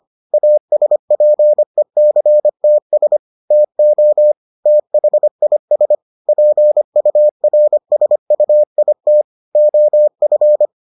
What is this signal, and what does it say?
Morse code, heard as a keyed tone, about 25 wpm, 600 Hz. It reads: ASPECTS TO THIS PURSUIT OF